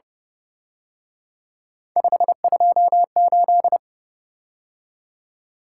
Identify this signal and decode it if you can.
Morse code, heard as a keyed tone, about 29 wpm, 700 Hz. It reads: 528
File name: Speed-Racing - Three Letter-Number Combinations - 25wpm